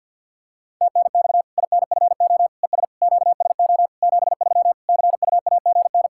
Morse code, heard as a keyed tone, about 51 wpm, 700 Hz. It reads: TNX INFO ES QSO 73 CUAGN
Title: Practice QSO - Exchange 4 - Ending - 50wpm